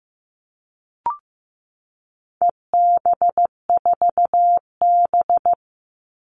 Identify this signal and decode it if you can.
Morse code, heard as a keyed tone, about 15 wpm, 700 Hz. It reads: EB4B
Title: All Cut Numbers - 4-Digit 15wpm